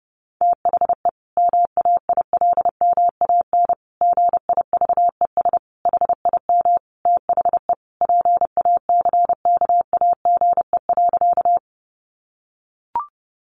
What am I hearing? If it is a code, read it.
Morse code, heard as a keyed tone, about 30 wpm, 700 Hz. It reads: THE MAILMAN GIVES HIM THE PACKAGE.
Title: Additional Easy Phrases 30wpm